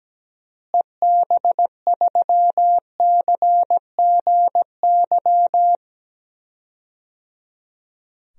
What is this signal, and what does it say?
Morse code, heard as a keyed tone, about 17 wpm, 700 Hz. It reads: EB3CGY
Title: DXCC-100 Call Signs - 6 Characters - Repeated 3x - 17wpm